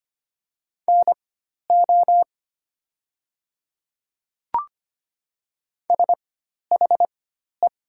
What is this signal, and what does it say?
Morse code, heard as a keyed tone, about 25 wpm, spaced as for 12 wpm, 700 Hz. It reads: NO SHE
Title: 4x Character Spacing - Top 100 Words - 25wpm